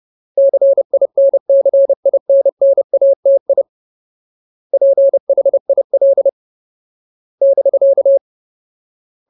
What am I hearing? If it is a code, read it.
Morse code, heard as a keyed tone, about 30 wpm, 550 Hz. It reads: CINCINNATI PHIL <BK>
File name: Ultimate Word List / 30WPM Ultimate